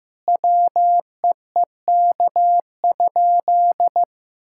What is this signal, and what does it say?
Morse code, heard as a keyed tone, about 15 wpm, 700 Hz. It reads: WEEK?
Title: Sentences from Top 1600 Words 15wpm